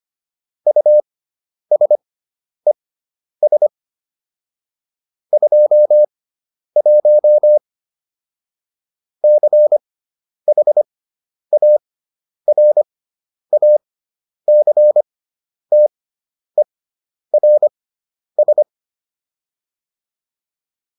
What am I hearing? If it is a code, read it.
Morse code, heard as a keyed tone, about 25 wpm, spaced as for 10 wpm, 600 Hz. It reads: USES 21 CHARACTERS